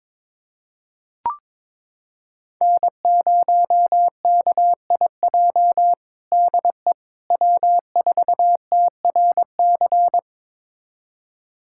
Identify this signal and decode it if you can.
Morse code, heard as a keyed tone, about 22 wpm, 700 Hz. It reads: N0KIJ DE W4TRC
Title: USA Callsign DE Callsign - 5 Characters - 22wpm